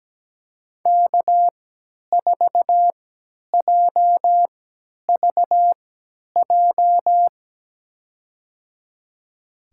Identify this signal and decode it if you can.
Morse code, heard as a keyed tone, about 17 wpm, spaced as for 10 wpm, 700 Hz. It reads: K4JVJ
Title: USA Call Signs - 5 Characters - 3x Character Spacing - 17wpm